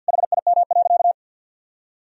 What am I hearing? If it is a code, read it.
Morse code, heard as a keyed tone, about 50 wpm, 700 Hz. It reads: HIM.
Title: Sentences from Top 1000 Words - Review Part 4 - 50wpm